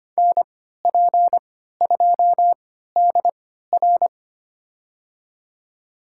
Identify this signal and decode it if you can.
Morse code, heard as a keyed tone, about 25 wpm, spaced as for 14 wpm, 700 Hz. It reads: NP2DR
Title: USA Call Signs - 5 Characters - 3x Character Spacing - 25wpm